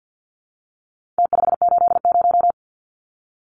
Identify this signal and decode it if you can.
Morse code, heard as a keyed tone, about 49 wpm, 700 Hz. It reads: T580
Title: Numbers - Contest Style - 4-Digit 50wpm